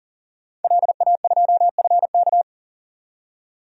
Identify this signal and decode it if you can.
Morse code, heard as a keyed tone, about 40 wpm, 700 Hz. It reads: LA2FK
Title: DXCC-100 Call Signs - 5 Characters - 40wpm